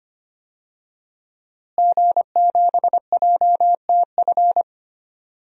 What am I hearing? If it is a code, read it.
Morse code, heard as a keyed tone, about 25 wpm, 700 Hz. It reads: G7JTF
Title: DXCC-100 Call Signs - 5 Characters - Repeated 3x - 25wpm